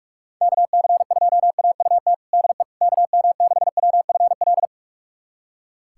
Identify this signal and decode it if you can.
Morse code, heard as a keyed tone, about 46 wpm, 700 Hz. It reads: KC2AUT DE KM6WFL
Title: USA Callsign DE Callsign - 6 Characters - 45wpm